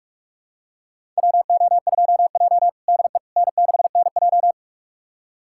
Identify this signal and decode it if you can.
Morse code, heard as a keyed tone, about 46 wpm, 700 Hz. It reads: WO2J DE N6NJ
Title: USA Callsign DE Callsign - 4 Characters - 45wpm